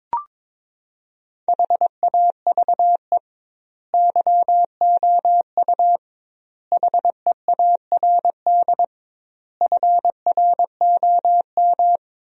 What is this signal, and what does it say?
Morse code, heard as a keyed tone, about 22 wpm, 700 Hz. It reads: HAVE YOU HEARD FROM
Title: 2x Word-Spacing - Sentences from Top 400 Words - 22wpm